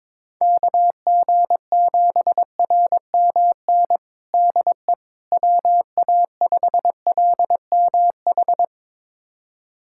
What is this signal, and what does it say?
Morse code, heard as a keyed tone, about 22 wpm, 700 Hz. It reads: KG7RMN DE WA5LMH
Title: USA Callsign DE Callsign - 6 Characters - 22wpm